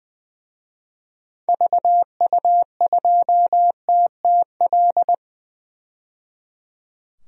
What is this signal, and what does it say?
Morse code, heard as a keyed tone, about 20 wpm, 700 Hz. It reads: VU2TTL
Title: DXCC-100 Call Signs - 6 Characters - 20wpm